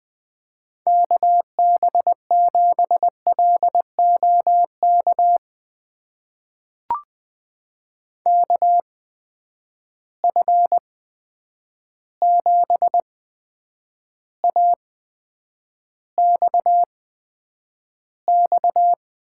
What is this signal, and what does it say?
Morse code, heard as a keyed tone, about 20 wpm, spaced as for 9 wpm, 700 Hz. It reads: KB7LOK KF7AXX
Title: USA Call Signs - 6 Characters - 8x Character Spacing - 20wpm